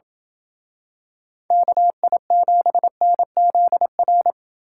Morse code, heard as a keyed tone, about 27 wpm, 700 Hz. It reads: KI7NZR